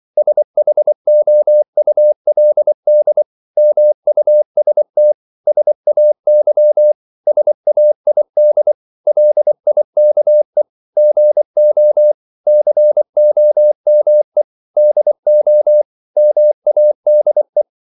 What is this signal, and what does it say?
Morse code, heard as a keyed tone, about 24 wpm, 600 Hz. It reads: SHOULD MUST SAY SAID LIKE GO COME DO MADE
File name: CW / Practice / 100MostCommonEnglishWords24